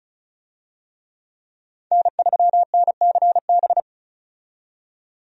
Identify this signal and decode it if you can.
Morse code, heard as a keyed tone, about 34 wpm, 700 Hz. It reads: N3NCB
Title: USA Call Signs - 5 Characters - 35wpm